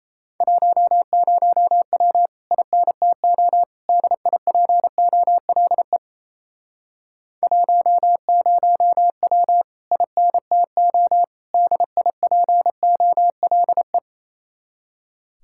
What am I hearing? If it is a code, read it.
Morse code, heard as a keyed tone, about 37 wpm, 700 Hz. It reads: 10W INTO DIPOLE 10W INTO DIPOLE